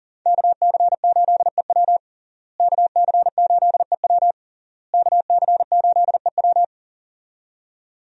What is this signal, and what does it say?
Morse code, heard as a keyed tone, about 40 wpm, 700 Hz. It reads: KC8EW KC8EW KC8EW
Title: USA Call Signs - 5 Characters - Repeated 3x - 40wpm